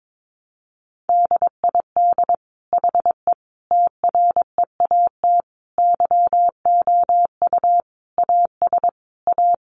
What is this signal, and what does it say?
Morse code, heard as a keyed tone, about 22 wpm, 700 Hz. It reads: DID HE TREAT YOU AS A
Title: Sentences from Top 1300 Words 22wpm